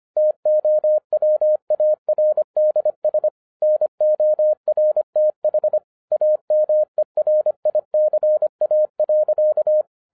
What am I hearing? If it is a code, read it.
Morse code, heard as a keyed tone, about 25 wpm, 610 Hz. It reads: TOWARDS NORTH AMERICA.